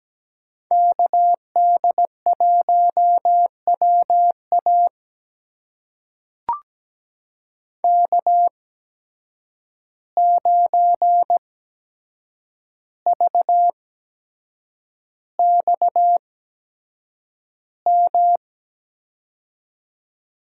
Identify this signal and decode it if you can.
Morse code, heard as a keyed tone, about 17 wpm, spaced as for 7 wpm, 700 Hz. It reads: KD1WA K9VXM
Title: USA Call Signs - 5 Characters - 8x Character Spacing - 17wpm